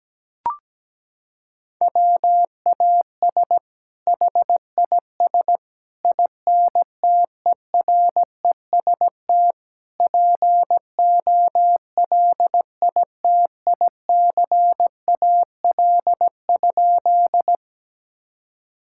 Morse code, heard as a keyed tone, about 17 wpm, 700 Hz. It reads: WAS HIS INTEREST POLITICAL?
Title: Sentences from Top 1000 Words - Review Part 5 - 17wpm